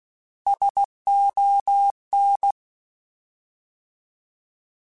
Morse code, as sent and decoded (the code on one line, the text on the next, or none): ... --- -.
SON